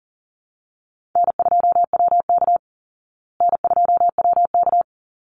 Morse code, heard as a keyed tone, about 37 wpm, 700 Hz. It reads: N2WK N2WK
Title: Call Signs Repeated 3x - 40wpm